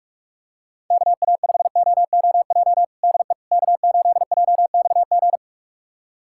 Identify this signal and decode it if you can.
Morse code, heard as a keyed tone, about 46 wpm, 700 Hz. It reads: KA5OOJ DE K8JXG
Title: USA Callsign DE Callsign 45wpm